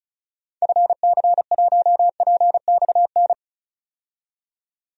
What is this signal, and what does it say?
Morse code, heard as a keyed tone, about 34 wpm, 700 Hz. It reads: FC1PXN
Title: DXCC-100 Call Signs - 6 Characters - 35wpm